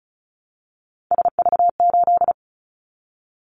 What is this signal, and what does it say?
Morse code, heard as a keyed tone, about 34 wpm, 700 Hz. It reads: SV8
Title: Callsign Prefixes for Top 50 DXCC - 35wpm